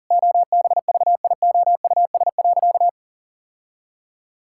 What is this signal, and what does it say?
Morse code, heard as a keyed tone, about 40 wpm, 700 Hz. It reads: OBVIOUS.
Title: Sentences from Top 1600 Words 40wpm